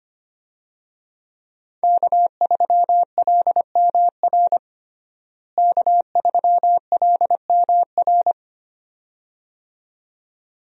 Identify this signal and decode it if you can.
Morse code, heard as a keyed tone, about 25 wpm, 700 Hz. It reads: K3LMR K3LMR
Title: USA Call Signs - 5 Characters - Repeated 2x - 25wpm